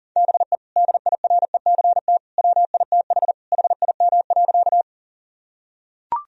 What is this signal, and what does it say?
Morse code, heard as a keyed tone, about 40 wpm, 700 Hz. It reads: BE DIRECT WITH HIM.
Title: Sentences from Top 1000 Words - Review Part 2 - 40wpm